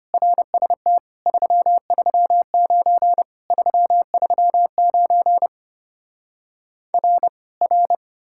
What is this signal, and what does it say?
Morse code, heard as a keyed tone, about 33 wpm, 700 Hz. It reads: RST 339 339 R R